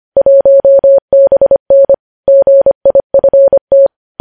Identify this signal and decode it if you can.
Morse code, heard as a keyed tone, about 25 wpm, 550 Hz. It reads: 1BN GIFT
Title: News Headlines at 25